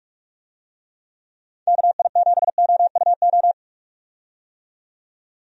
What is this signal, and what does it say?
Morse code, heard as a keyed tone, about 46 wpm, 700 Hz. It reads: KI7OUO
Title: USA Call Signs - 6 Characters - 45wpm